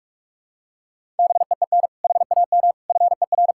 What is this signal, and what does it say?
Morse code, heard as a keyed tone, about 41 wpm, 700 Hz. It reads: BEEN HAM FER
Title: Practice QSO - Exchange 3 - More Info - 45wpm